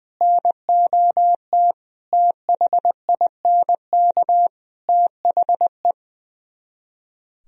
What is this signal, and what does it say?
Morse code, heard as a keyed tone, about 20 wpm, 700 Hz. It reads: NOT THINK THE
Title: Speed-Racing - Sets of 4 Words 20wpm